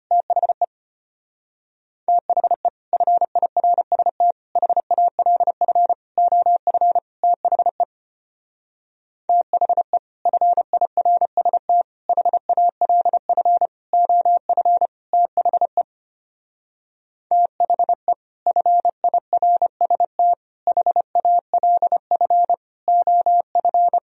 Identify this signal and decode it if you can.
Morse code, heard as a keyed tone, about 29 wpm, 700 Hz. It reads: THE THE FIRST HALF OF THE THE FIRST HALF OF THE THE FIRST HALF OF
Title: Speed-Racing - Sets of 5 Words 25wpm